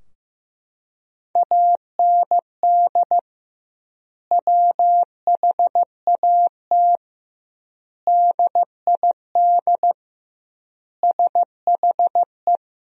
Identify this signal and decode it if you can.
Morse code, heard as a keyed tone, about 15 wpm, 700 Hz. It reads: AND WHAT DID SHE
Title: 2x Word-Spacing - Sentences from Top 200 Words - 15wpm